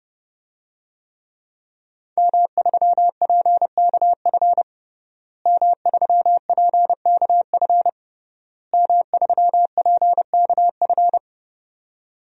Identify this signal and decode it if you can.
Morse code, heard as a keyed tone, about 30 wpm, 700 Hz. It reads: M3PKF M3PKF M3PKF
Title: DXCC-100 Call Signs - 5 Characters - Repeated 3x - 30wpm